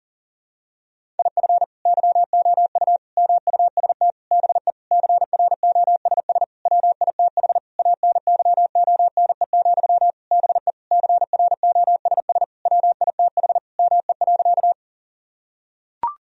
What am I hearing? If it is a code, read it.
Morse code, heard as a keyed tone, about 40 wpm, 700 Hz. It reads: IF YOU MUST BE CROSS WITH ANYONE, BE CROSS WITH ME.
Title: Sentences from Top 1000 Words - Review Part 1 - 40wpm